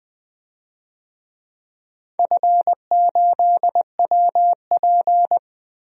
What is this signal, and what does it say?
Morse code, heard as a keyed tone, about 20 wpm, 700 Hz. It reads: F8WP
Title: DXCC-100 Call Signs - 4 Characters - 20wpm